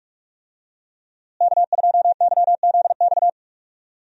Morse code, heard as a keyed tone, about 41 wpm, 700 Hz. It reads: K2YZX
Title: USA Call Signs - 5 Characters - Repeated 3x - 45wpm